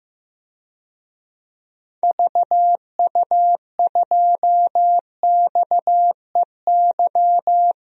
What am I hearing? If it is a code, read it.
Morse code, heard as a keyed tone, about 15 wpm, 700 Hz. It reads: VU2XEY
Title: DXCC-100 Call Signs - 6 Characters - Repeated 2x - 15wpm